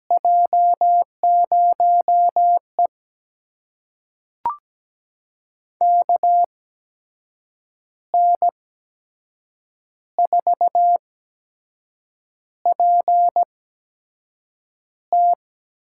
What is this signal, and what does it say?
Morse code, heard as a keyed tone, about 17 wpm, spaced as for 6 wpm, 700 Hz. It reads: J0E KN4PT
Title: USA Call Signs - 8x Character Spacing - 17wpm